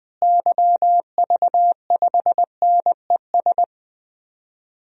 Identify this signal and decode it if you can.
Morse code, heard as a keyed tone, about 20 wpm, 700 Hz. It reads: YV5NES
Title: DXCC-100 Call Signs - 6 Characters - Repeated 2x - 20wpm